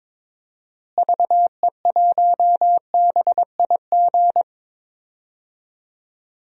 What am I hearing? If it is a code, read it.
Morse code, heard as a keyed tone, about 22 wpm, 700 Hz. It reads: VE1BIG
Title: DXCC-100 Call Signs - 6 Characters - Repeated 3x - 22wpm